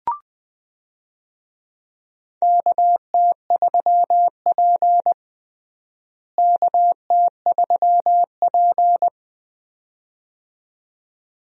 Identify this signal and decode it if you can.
Morse code, heard as a keyed tone, about 20 wpm, 700 Hz. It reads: KT3P KT3P
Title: USA Call Signs - 4 Characters - Repeated 2x - 20wpm